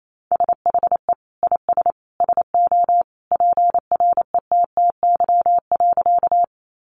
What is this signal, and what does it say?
Morse code, heard as a keyed tone, about 28 wpm, 700 Hz. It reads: SHE IS SO PRETTY.
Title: Sentences from Top 1000 Words 28wpm